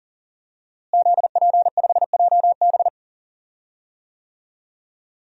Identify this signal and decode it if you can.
Morse code, heard as a keyed tone, about 40 wpm, 700 Hz. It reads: ZP5JB
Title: DXCC-100 Call Signs - 5 Characters - Repeated 2x - 40wpm